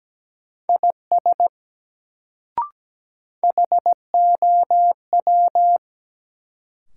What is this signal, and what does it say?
Morse code, heard as a keyed tone, about 17 wpm, 700 Hz. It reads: IS HOW
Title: Sentences from Top 100 Words - New Word Drill - 17wpm